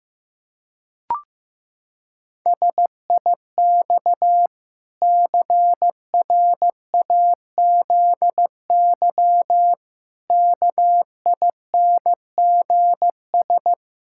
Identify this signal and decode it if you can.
Morse code, heard as a keyed tone, about 15 wpm, 700 Hz. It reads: SIX CRAZY KINGS
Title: Pangram Challenge v2 - 15wpm